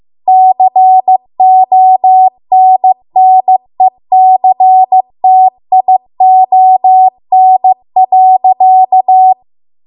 Morse code, heard as a keyed tone, about 15 wpm, 750 Hz. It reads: CONNECTION.